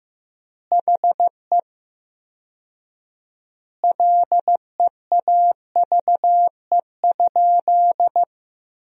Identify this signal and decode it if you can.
Morse code, heard as a keyed tone, about 15 wpm, 700 Hz. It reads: HE LEAVE?
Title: 4x Word-Spacing - Sentences from Top 400 Words - 15wpm